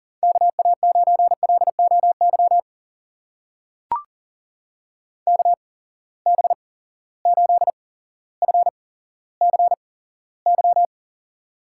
Morse code, heard as a keyed tone, about 39 wpm, spaced as for 17 wpm, 700 Hz. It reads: KA9LOY KB8FCY
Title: USA Call Signs - 6 Characters - 8x Character Spacing - 40wpm